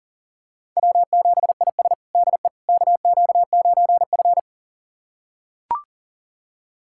Morse code, heard as a keyed tone, about 40 wpm, 700 Hz. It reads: W7IS DE KQ9F
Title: USA Callsign DE Callsign - 4 Characters - 40wpm